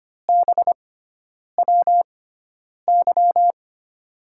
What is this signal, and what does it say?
Morse code, heard as a keyed tone, about 25 wpm, spaced as for 9 wpm, 700 Hz. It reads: BWY
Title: USA Call Signs - 6 Characters - 6x Character Spacing - 25wpm